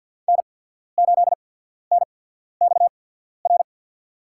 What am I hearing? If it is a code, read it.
Morse code, heard as a keyed tone, about 53 wpm, spaced as for 14 wpm, 700 Hz. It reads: N8NXR
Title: USA Call Signs - 5 Characters - 8x Character Spacing - 50wpm